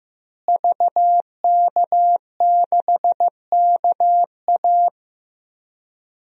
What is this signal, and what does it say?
Morse code, heard as a keyed tone, about 15 wpm, 700 Hz. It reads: VK6KA
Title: DXCC-100 Call Signs - 5 Characters - Repeated 2x - 15wpm